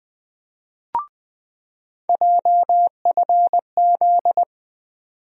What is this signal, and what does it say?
Morse code, heard as a keyed tone, about 20 wpm, 700 Hz. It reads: JFZ
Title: USA Call Sign Postfixes - 3 Letters 20wpm